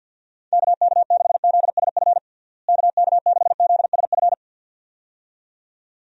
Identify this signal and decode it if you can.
Morse code, heard as a keyed tone, about 51 wpm, 700 Hz. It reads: KK6ZSF KK6ZSF
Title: USA Call Signs - 6 Characters - Repeated 2x - 50wpm